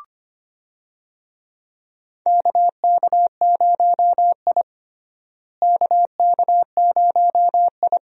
Morse code, heard as a keyed tone, about 25 wpm, 700 Hz. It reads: KK0I KK0I
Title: USA Call Signs - 4 Characters - Repeated 3x - 25wpm